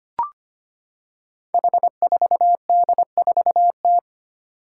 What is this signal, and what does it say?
Morse code, heard as a keyed tone, about 25 wpm, 700 Hz. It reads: H4D4T
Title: Five Challenging Letter-Number Combinations 25wpm